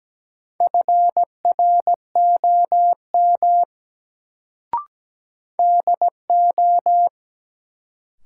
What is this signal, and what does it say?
Morse code, heard as a keyed tone, about 17 wpm, 700 Hz. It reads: FROM DO
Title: Sets of 2 Words - New Word Drill - 17wpm